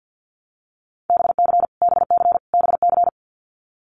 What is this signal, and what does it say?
Morse code, heard as a keyed tone, about 50 wpm, 700 Hz. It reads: BC BC BC
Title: Canadian Provinces and Territories - Repeated 3x - 50wpm